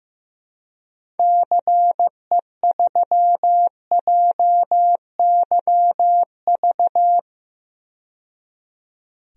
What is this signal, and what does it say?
Morse code, heard as a keyed tone, about 15 wpm, 700 Hz. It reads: CE3JYV
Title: DXCC-100 Call Signs - 6 Characters - 15wpm